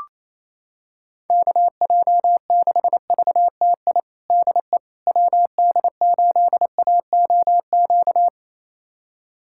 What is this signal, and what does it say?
Morse code, heard as a keyed tone, about 27 wpm, 700 Hz. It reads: KJ6VTI DE WD8AOQ
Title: USA Callsign DE Callsign - 6 Characters - 28wpm